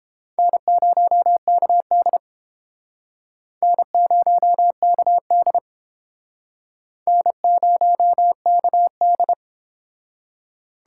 Morse code, heard as a keyed tone, about 30 wpm, 700 Hz. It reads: N0KD N0KD N0KD